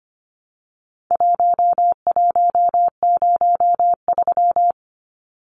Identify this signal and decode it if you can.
Morse code, heard as a keyed tone, about 25 wpm, 700 Hz. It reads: 1103
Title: Numbers - Contest Style - 4-Digit 25wpm